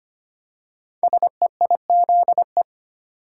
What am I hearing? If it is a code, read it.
Morse code, heard as a keyed tone, about 25 wpm, 700 Hz. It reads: SEIZE
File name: 5 Letter Words - First 3 Letters Given - 25wpm